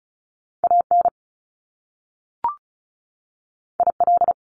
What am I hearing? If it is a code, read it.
Morse code, heard as a keyed tone, about 36 wpm, 700 Hz. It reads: AN IL
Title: Top Two Letter Combinations in Words 35wpm